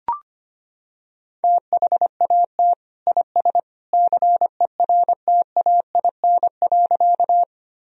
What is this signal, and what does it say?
Morse code, heard as a keyed tone, about 25 wpm, 700 Hz. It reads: THAT IS CERTAIN.
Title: Sentences from Top 1000 Words - Review Part 2 - 25wpm